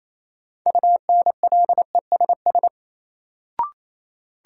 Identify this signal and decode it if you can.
Morse code, heard as a keyed tone, about 28 wpm, 700 Hz. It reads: UNLESS
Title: Sentences from Top 1300 Words - New Word Drill Alt - 28wpm